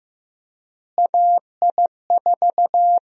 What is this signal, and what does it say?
Morse code, heard as a keyed tone, about 15 wpm, 700 Hz. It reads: AI4
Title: USA Call Sign Prefixes - Repeated 2x - 15wpm